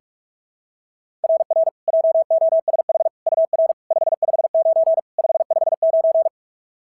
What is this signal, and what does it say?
Morse code, heard as a keyed tone, about 44 wpm, 690 Hz. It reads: RR JOSH UR 559 559
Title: SOTA QSO Practice 45wpm